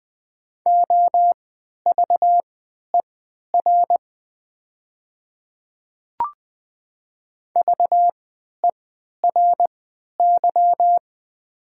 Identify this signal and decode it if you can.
Morse code, heard as a keyed tone, about 20 wpm, spaced as for 11 wpm, 700 Hz. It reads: OVER VERY
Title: 3x Character Spacing - Top 200 Words - 20wpm